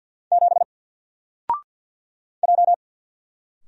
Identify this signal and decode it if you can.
Morse code, heard as a keyed tone, about 48 wpm, 700 Hz. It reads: 7 J
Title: Single Letter-Number Alt 50wpm